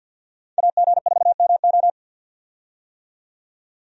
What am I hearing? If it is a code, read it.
Morse code, heard as a keyed tone, about 50 wpm, 700 Hz. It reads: AG4MO
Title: USA Call Signs - 5 Characters - 50wpm